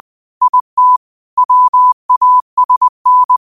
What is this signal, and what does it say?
Morse code, heard as a keyed tone, about 20 wpm, 1000 Hz. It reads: IT WASN
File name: qotdfiles / 20wpm